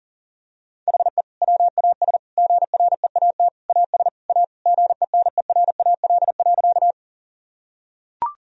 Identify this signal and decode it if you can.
Morse code, heard as a keyed tone, about 40 wpm, 700 Hz. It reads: HE WAS GREAT AS A GENERAL.
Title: Sentences from Top 1000 Words - Review Part 3 - 40wpm